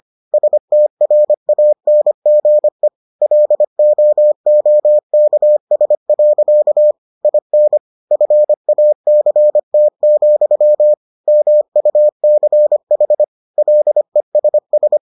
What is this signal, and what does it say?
Morse code, heard as a keyed tone, about 25 wpm, 600 Hz. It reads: STRANGE LOOKS. IN FACT, MUCH LESS